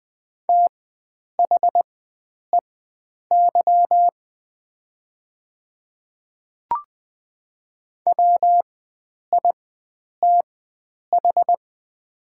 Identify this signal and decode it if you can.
Morse code, heard as a keyed tone, about 20 wpm, spaced as for 9 wpm, 700 Hz. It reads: THEY WITH
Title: 4x Character Spacing - Top 100 Words - 20wpm